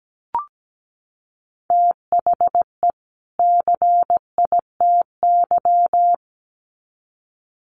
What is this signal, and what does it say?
Morse code, heard as a keyed tone, about 17 wpm, 700 Hz. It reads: THE CITY
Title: Sets of 2 Words 17wpm